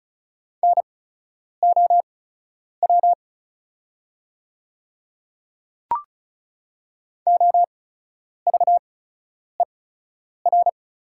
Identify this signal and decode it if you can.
Morse code, heard as a keyed tone, about 36 wpm, spaced as for 10 wpm, 690 Hz. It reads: NOW OVER
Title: 8x Character Spacing - Top 200 Words - 35wpm